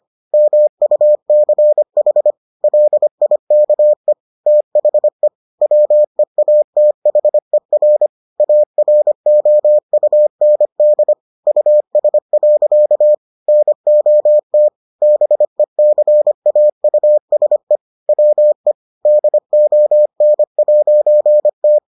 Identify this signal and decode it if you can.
Morse code, heard as a keyed tone, about 25 wpm, 600 Hz. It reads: MUCH LIKE THE WEATHER AROUND US. NOT BECAUSE WE DON'T